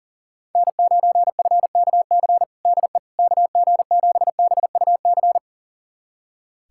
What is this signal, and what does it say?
Morse code, heard as a keyed tone, about 40 wpm, 700 Hz. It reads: N9FKC DE KG7BUC
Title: USA Callsign DE Callsign 40wpm